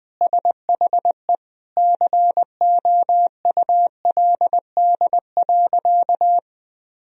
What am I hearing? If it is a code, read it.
Morse code, heard as a keyed tone, about 20 wpm, 700 Hz. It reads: SHE COULD.
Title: Sentences from Top 1000 Words - Review Part 3 - 20wpm